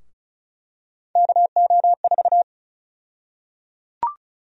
Morse code, heard as a keyed tone, about 36 wpm, 700 Hz. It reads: KO4
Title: USA Call Sign Prefixes - Repeated 2x - 35wpm